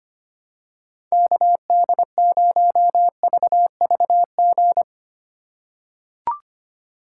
Morse code, heard as a keyed tone, about 25 wpm, 700 Hz. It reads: KD0VVG
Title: USA Call Signs - 6 Characters - 2x Character Spacing - 25wpm